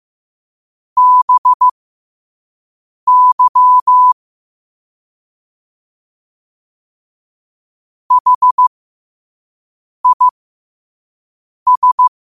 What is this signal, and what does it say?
Morse code, heard as a keyed tone, about 15 wpm, spaced as for 5 wpm, 1000 Hz. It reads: BY HIS
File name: 5wpm